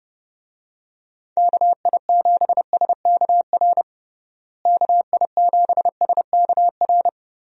Morse code, heard as a keyed tone, about 32 wpm, 700 Hz. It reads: KI7SKR KI7SKR
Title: USA Call Signs - 6 Characters - Repeated 2x - 30wpm